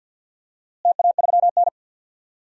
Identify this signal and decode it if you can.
Morse code, heard as a keyed tone, about 52 wpm, 700 Hz. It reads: TA3N
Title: DXCC-100 Call Signs - 4 Characters - Repeated 2x - 50wpm